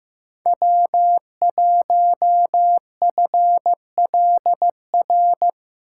Morse code, heard as a keyed tone, about 15 wpm, 700 Hz. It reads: W1FLR
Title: USA Call Signs - 5 Characters - 3x Character Spacing - 15wpm